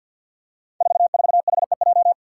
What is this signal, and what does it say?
Morse code, heard as a keyed tone, about 46 wpm, 700 Hz. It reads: 44HEJ
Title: Five Challenging Letter-Number Combinations 50wpm